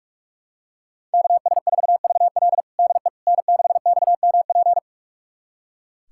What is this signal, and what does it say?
Morse code, heard as a keyed tone, about 41 wpm, 700 Hz. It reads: KS4VL DE N6XMP
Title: USA Callsign DE Callsign - 5 Characters - 45wpm